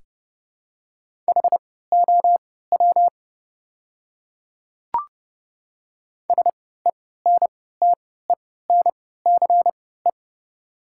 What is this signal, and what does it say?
Morse code, heard as a keyed tone, about 30 wpm, spaced as for 17 wpm, 700 Hz. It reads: HOW SENTENCE
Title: 3x Character Spacing - Top 200 Words - 30wpm